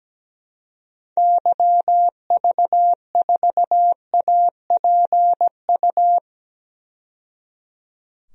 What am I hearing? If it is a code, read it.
Morse code, heard as a keyed tone, about 17 wpm, 700 Hz. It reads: YV4APU
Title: DXCC-100 Call Signs - 6 Characters - 17wpm